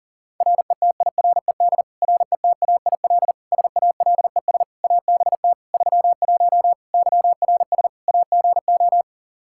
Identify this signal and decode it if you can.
Morse code, heard as a keyed tone, about 40 wpm, 700 Hz. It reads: RETIRED RETAIL SALES ABT 31 YRS AGO
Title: Practice QSO - Exchange 3 - More Info - 40wpm